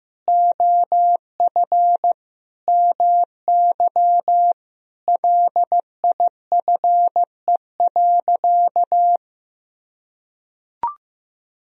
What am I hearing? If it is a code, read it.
Morse code, heard as a keyed tone, about 15 wpm, 700 Hz. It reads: OF MY LIFE.
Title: Sentences from Top 1000 Words - Review Part 1 - 15wpm